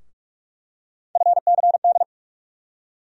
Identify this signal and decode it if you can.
Morse code, heard as a keyed tone, about 42 wpm, 700 Hz. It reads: FCD
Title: USA Call Sign Postfixes - 3 Letters 45wpm